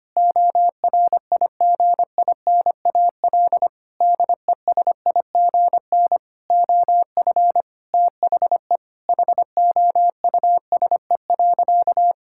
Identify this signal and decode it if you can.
Morse code, heard as a keyed tone, about 25 wpm, 700 Hz. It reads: ORIGINAL DESIGN OF THE HOUSE.
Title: Sentences from Top 1000 Words - Review Part 6 - 25wpm